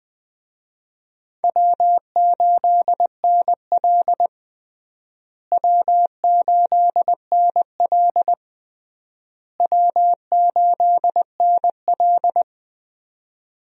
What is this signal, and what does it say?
Morse code, heard as a keyed tone, about 21 wpm, 700 Hz. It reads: W8NL W8NL W8NL
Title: USA Call Signs - 4 Characters - Repeated 3x - 20wpm